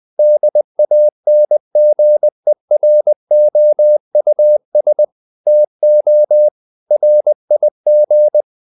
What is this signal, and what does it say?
Morse code, heard as a keyed tone, about 20 wpm, 600 Hz. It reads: DANGEROUS TO RIG
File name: the old man and the sea part 2 20wpm